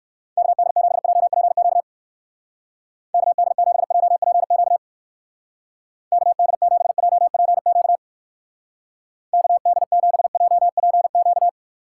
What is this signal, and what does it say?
Morse code, heard as a keyed tone, about 56 wpm, 700 Hz. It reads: KD7JPQ KD7JPQ KD7JPQ KD7JPQ
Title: Speed Racing - USA 2x3 Call Signs 45wpm